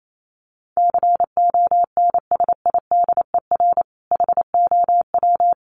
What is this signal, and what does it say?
Morse code, heard as a keyed tone, about 28 wpm, 700 Hz. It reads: CONSIDER HOW
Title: Sentences from Top 1000 Words 28wpm